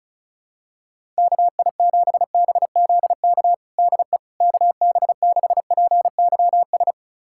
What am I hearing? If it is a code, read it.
Morse code, heard as a keyed tone, about 36 wpm, 700 Hz. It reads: KI7BZK DE KB6PYS
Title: USA Callsign DE Callsign - 6 Characters - 35wpm